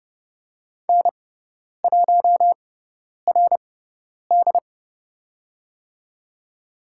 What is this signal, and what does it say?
Morse code, heard as a keyed tone, about 30 wpm, spaced as for 10 wpm, 700 Hz. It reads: N1RD
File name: Lesson 16g - Letter c - Call Signs 30wpm